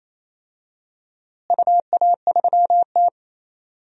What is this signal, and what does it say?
Morse code, heard as a keyed tone, about 28 wpm, 700 Hz. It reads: UA3T
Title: DXCC-100 Call Signs - 4 Characters - Repeated 3x - 28wpm